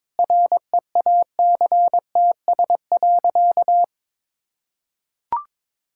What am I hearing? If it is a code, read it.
Morse code, heard as a keyed tone, about 22 wpm, 700 Hz. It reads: REACTS.
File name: practice / 22wpm / Sentences with 8-Word Phrases - 22wpm